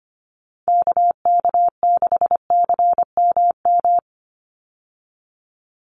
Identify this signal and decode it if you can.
Morse code, heard as a keyed tone, about 25 wpm, 700 Hz. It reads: KK6CMM
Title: USA Call Signs 25wpm